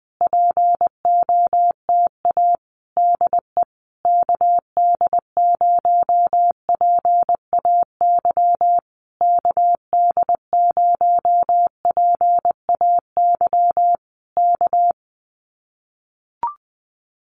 Morse code, heard as a keyed tone, about 20 wpm, 700 Hz. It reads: POTA DE KD0PAY KD0PAY K
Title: Common QSO Phrases Encore 20wpm